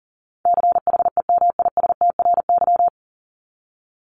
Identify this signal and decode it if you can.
Morse code, heard as a keyed tone, about 40 wpm, 700 Hz. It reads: CHEMISTRY
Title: Top 751-1000 Words 40wpm